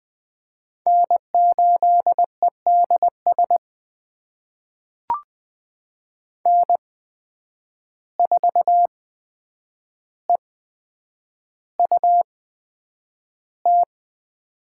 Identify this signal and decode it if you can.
Morse code, heard as a keyed tone, about 20 wpm, spaced as for 9 wpm, 700 Hz. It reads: N8EDS N4EUT
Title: USA Call Signs - 5 Characters - 8x Character Spacing - 20wpm